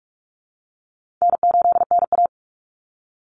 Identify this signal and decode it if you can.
Morse code, heard as a keyed tone, about 44 wpm, 700 Hz. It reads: N8NA
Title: Call Signs - 45wpm